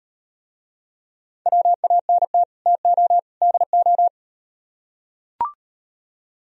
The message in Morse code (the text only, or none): WANT TO DO